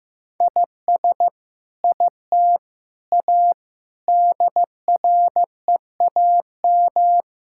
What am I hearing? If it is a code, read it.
Morse code, heard as a keyed tone, about 15 wpm, 700 Hz. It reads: IS IT A DREAM
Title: Sentences from Top 1600 Words 15wpm